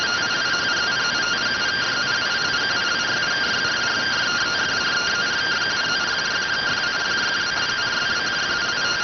Two-tone teleprinter signal, about 170 Hz shift.